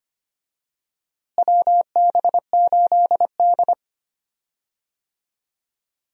Morse code, heard as a keyed tone, about 25 wpm, 700 Hz. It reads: WB8D
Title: USA Call Signs - 4 Characters - Repeated 3x - 25wpm